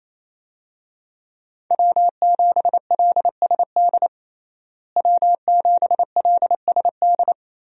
Morse code, heard as a keyed tone, about 28 wpm, 700 Hz. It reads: W7LSD W7LSD
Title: USA Call Signs - 5 Characters - Repeated 3x - 28wpm